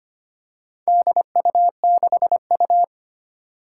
Morse code, heard as a keyed tone, about 25 wpm, 700 Hz. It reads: DU6U
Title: Four Challenging Letter-Number Combinations 25wpm